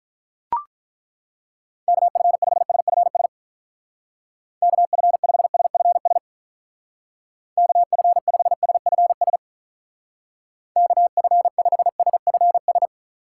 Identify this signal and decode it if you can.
Morse code, heard as a keyed tone, about 42 wpm, 700 Hz. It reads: KF5SFS KF5SFS KF5SFS KF5SFS
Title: Speed Racing - USA 2x3 Call Signs 35wpm